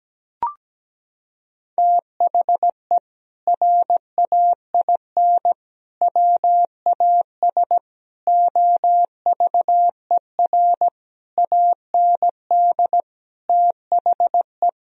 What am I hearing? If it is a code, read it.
Morse code, heard as a keyed tone, about 17 wpm, 700 Hz. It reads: THE RAIN WAS OVER AND THE
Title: Sentences from Top 1000 Words - Review Part 1 - 17wpm